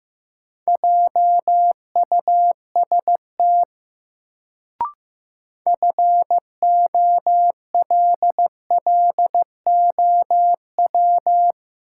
Morse code, heard as a keyed tone, about 16 wpm, 700 Hz. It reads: JUST FOLLOW
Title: Sentences from Top 200 Words - New Word Drill - 15wpm